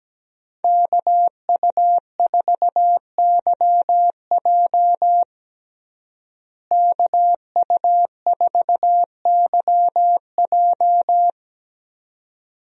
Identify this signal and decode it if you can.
Morse code, heard as a keyed tone, about 17 wpm, 700 Hz. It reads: KU4YJ KU4YJ
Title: USA Call Signs - 5 Characters - Repeated 2x - 17wpm